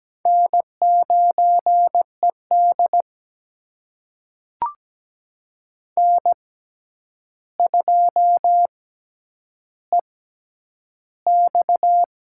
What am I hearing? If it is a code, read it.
Morse code, heard as a keyed tone, about 17 wpm, spaced as for 9 wpm, 700 Hz. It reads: N9ED N2EX
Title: USA Call Signs - 4 Characters - 6x Character Spacing - 17wpm